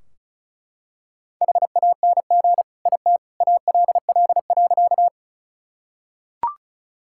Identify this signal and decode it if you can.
Morse code, heard as a keyed tone, about 36 wpm, 700 Hz. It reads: HANG IT ALL.
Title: Sentences from Top 1600 Words 35wpm